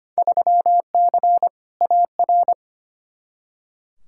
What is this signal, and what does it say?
Morse code, heard as a keyed tone, about 25 wpm, 700 Hz. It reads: 3C AR
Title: Field Day Exchanges Encore - 25wpm